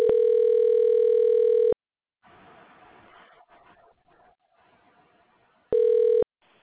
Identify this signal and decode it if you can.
Ringback tone (North America), 440 + 480 Hz.